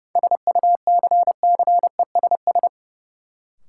Morse code, heard as a keyed tone, about 30 wpm, 700 Hz. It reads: SUCCESS
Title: Sentences from Top 900 Words - New Word Drill - 30wpm